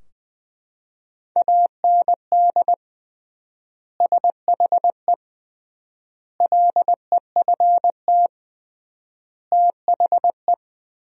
Morse code, heard as a keyed tone, about 20 wpm, 700 Hz. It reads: AND SHE LEFT THE
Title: 3x Word-Spacing - Sentences from Top 300 Words - 20wpm